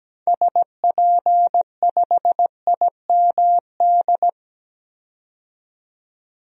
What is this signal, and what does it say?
Morse code, heard as a keyed tone, about 17 wpm, 700 Hz. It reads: SP5IMD
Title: DXCC-100 Call Signs - 6 Characters - 17wpm